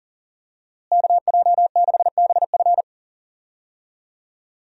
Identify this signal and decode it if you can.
Morse code, heard as a keyed tone, about 37 wpm, 700 Hz. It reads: KJ6BF
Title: USA Call Signs - 5 Characters - Repeated 2x - 40wpm